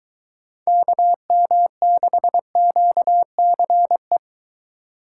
Morse code, heard as a keyed tone, about 23 wpm, 700 Hz. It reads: KM6QCE